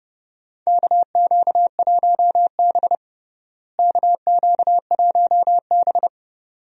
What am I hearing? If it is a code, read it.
Morse code, heard as a keyed tone, about 29 wpm, 700 Hz. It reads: KQ1B KQ1B
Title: USA Call Signs - 4 Characters - Repeated 2x - 30wpm